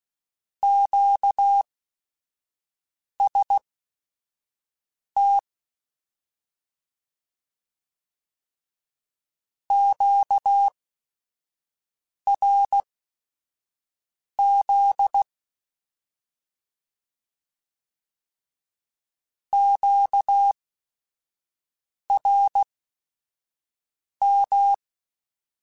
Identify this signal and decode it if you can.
Morse code, heard as a keyed tone, about 16 wpm, spaced as for 5 wpm, 800 Hz. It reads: QST QRZ QRM
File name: Short Runs / Short Runs-mp3s / Words @5